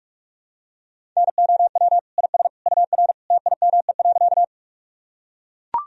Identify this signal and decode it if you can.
Morse code, heard as a keyed tone, about 44 wpm, 700 Hz. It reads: NOW IS UR TIME.
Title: Sentences from Top 100 Words and Abbreviations 45wpm